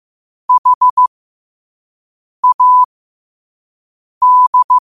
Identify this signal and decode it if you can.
Morse code, heard as a keyed tone, about 15 wpm, spaced as for 5 wpm, 1000 Hz. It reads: HAD
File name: qotdfiles / 5wpm